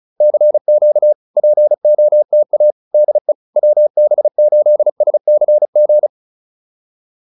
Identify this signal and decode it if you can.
Morse code, heard as a keyed tone, about 34 wpm, 600 Hz. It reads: CQ POTA DE WB8SCG